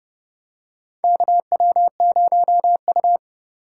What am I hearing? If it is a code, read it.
Morse code, heard as a keyed tone, about 31 wpm, 700 Hz. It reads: KW0U
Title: USA Call Signs - 4 Characters - 3x Character Spacing - 30wpm